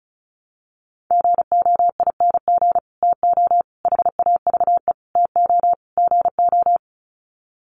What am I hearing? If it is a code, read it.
Morse code, heard as a keyed tone, about 34 wpm, 700 Hz. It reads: GOING TO HAVE TO GO
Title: Sets of 5 Words Encore 35wpm